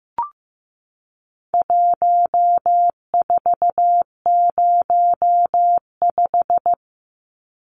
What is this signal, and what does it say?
Morse code, heard as a keyed tone, about 15 wpm, 700 Hz. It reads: 1405
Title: Numbers - 4-Digit 15wpm